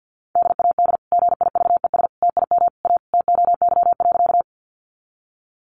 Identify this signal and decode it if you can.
Morse code, heard as a keyed tone, about 50 wpm, 700 Hz. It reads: DAD GIVES TIM A TOY.
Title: Easy Phrases 50wpm